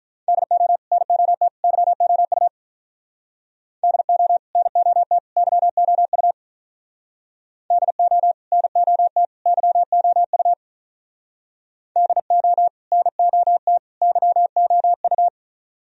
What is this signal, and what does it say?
Morse code, heard as a keyed tone, about 44 wpm, 700 Hz. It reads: DO NOT YOU DO NOT YOU DO NOT YOU DO NOT YOU
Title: Speed-Racing - Sets of 3 Words 35wpm